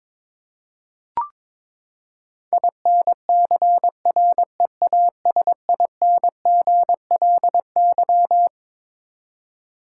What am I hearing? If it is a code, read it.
Morse code, heard as a keyed tone, about 23 wpm, 700 Hz. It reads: INCREASINGLY
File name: Twelve Letter Words - 22wpm